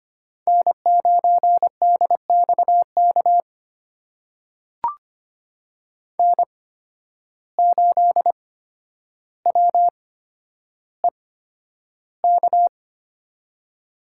Morse code, heard as a keyed tone, about 25 wpm, spaced as for 11 wpm, 700 Hz. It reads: N9DXK N8WEK
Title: USA Call Signs - 5 Characters - 8x Character Spacing - 25wpm